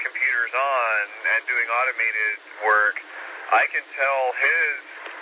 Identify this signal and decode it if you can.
Emergency Alert System (SAME) tones, heard as harsh buzzing bursts, with speech. It is heard from 0.9 to 5.0 s.